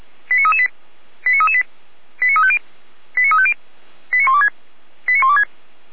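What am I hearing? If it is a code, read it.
Five-tone selective call (ZVEI), heard as quick runs of five tones, tones of about 70 ms. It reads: 89338 89338 89350 89350 89126 89126
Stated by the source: Received on SDR; ZVEI five-tone call, feuerwehr-orlamuende.org (fire brigade Orlamünde).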